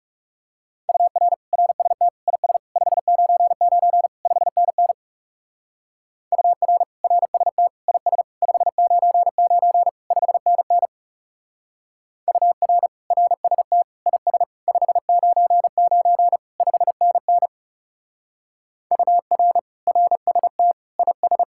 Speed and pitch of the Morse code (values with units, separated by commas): 39 wpm, 700 Hz